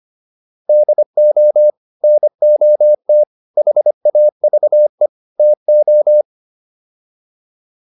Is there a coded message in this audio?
Yes: Morse code from 0.7 to 6.2 s, heard as a keyed tone, about 25 wpm, 600 Hz. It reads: DO NOT HAVE TO